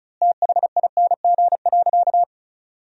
Morse code, heard as a keyed tone, about 34 wpm, 700 Hz. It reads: THING.
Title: 4x Word-Spacing - Sentences from Top 400 Words - 35wpm